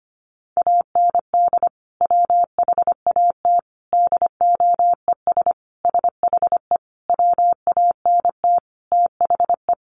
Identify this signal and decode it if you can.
Morse code, heard as a keyed tone, about 25 wpm, 700 Hz. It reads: AND WHAT DOES SHE WANT THE
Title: Sentences from Top 400 Words 25wpm